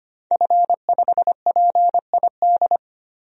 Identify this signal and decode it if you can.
Morse code, heard as a keyed tone, about 25 wpm, 700 Hz. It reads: F5PID
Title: DXCC-100 Call Signs - 5 Characters - 25wpm